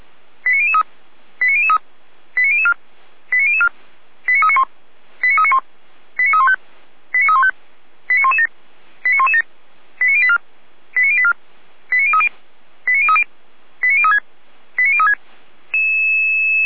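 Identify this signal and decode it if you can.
Five-tone selective call (ZVEI), heard as quick runs of five tones, tones of about 70 ms. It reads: 89003 89003 89004 89004 89391 89391 89315 89315 89117 89117 89084 89084 89030 89030 89036 89036
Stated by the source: ZVEI five-tone selective call received on SDR.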